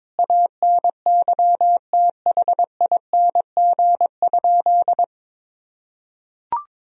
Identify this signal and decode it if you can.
Morse code, heard as a keyed tone, about 22 wpm, 700 Hz. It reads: ANYTHING?
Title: Sentences from Top 1000 Words - Review Part 2 - 22wpm